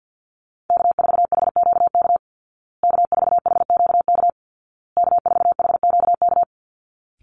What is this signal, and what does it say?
Morse code, heard as a keyed tone, about 50 wpm, 700 Hz. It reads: K4HQK K4HQK K4HQK
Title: Call Signs Repeated 2x - 50wpm